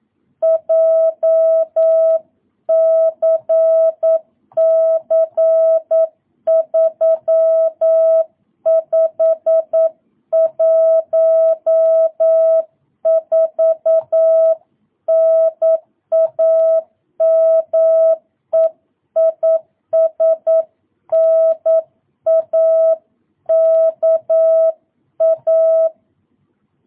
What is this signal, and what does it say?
Morse code, heard as a keyed tone, about 9 wpm, 650 Hz. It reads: JCC3514 NAME IS NAKA